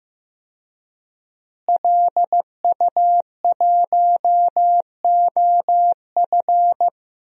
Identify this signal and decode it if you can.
Morse code, heard as a keyed tone, about 16 wpm, 700 Hz. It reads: LU1OF
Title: DXCC-100 Call Signs - 5 Characters - Repeated 3x - 15wpm